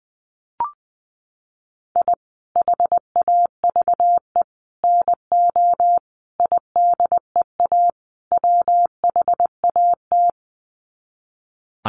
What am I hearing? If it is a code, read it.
Morse code, heard as a keyed tone, about 20 wpm, 700 Hz. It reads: I HAVE NO IDEA WHAT
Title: Sets of 5 Words Encore 20wpm